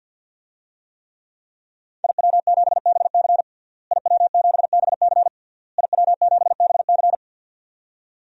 Morse code, heard as a keyed tone, about 52 wpm, 700 Hz. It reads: IW7BC IW7BC IW7BC
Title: DXCC-100 Call Signs - 5 Characters - Repeated 3x - 50wpm